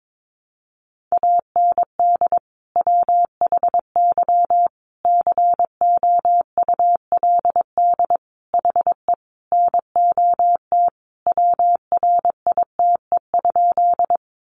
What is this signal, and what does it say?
Morse code, heard as a keyed tone, about 22 wpm, 700 Hz. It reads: AND WHY COULD HE NOT WRITE?
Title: Sentences from Top 600 Words 22wpm